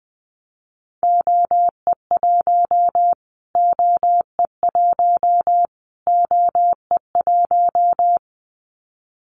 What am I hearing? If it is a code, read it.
Morse code, heard as a keyed tone, about 20 wpm, 700 Hz. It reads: OE1 OE1 OE1
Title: Callsign Prefixes for Top 25 DXCC - Repeated 3x - 20wpm